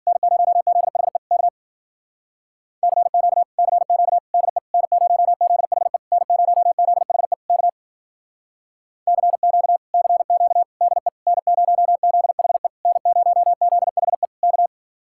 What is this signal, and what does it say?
Morse code, heard as a keyed tone, about 49 wpm, 700 Hz. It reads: N0ZHE K CQ CQ DE N0ZHE N0ZHE K CQ CQ DE N0ZHE N0ZHE K
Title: Speed-Racing Common QSO Phrases - 40wpm